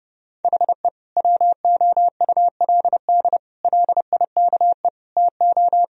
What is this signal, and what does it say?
Morse code, heard as a keyed tone, about 30 wpm, 700 Hz. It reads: HE WOULD LIKE TO
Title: Sentences from Top 1000 Words - Review Part 3 - 30wpm